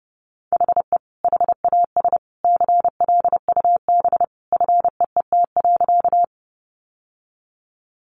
Morse code, heard as a keyed tone, about 30 wpm, 700 Hz. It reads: HE HAS CLUB FEET.
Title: Sentences from Top 1200 Words 30wpm